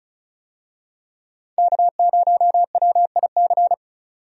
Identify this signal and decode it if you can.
Morse code, heard as a keyed tone, about 34 wpm, 700 Hz. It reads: K0WIC
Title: USA Call Signs - 5 Characters - 35wpm